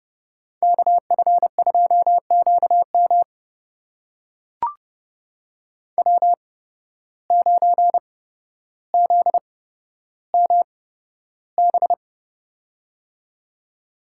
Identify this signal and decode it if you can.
Morse code, heard as a keyed tone, about 31 wpm, spaced as for 13 wpm, 700 Hz. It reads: KF2QM W9ZMB
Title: USA Call Signs - 5 Characters - 8x Character Spacing - 30wpm